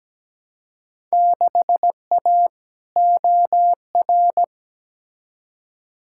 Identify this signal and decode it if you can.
Morse code, heard as a keyed tone, about 17 wpm, 700 Hz. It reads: 6A OR
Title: Field Day Exchanges - 17wpm